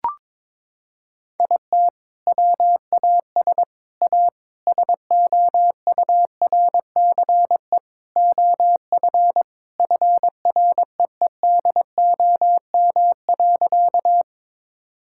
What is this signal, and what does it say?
Morse code, heard as a keyed tone, about 22 wpm, 700 Hz. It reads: IT WAS A SOURCE OF FREEDOM.
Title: Sentences from Top 1000 Words - Review Part 5 - 22wpm